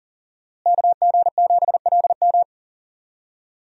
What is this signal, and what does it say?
Morse code, heard as a keyed tone, about 40 wpm, 700 Hz. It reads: KG7LM
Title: USA Call Signs - 5 Characters - Repeated 2x - 40wpm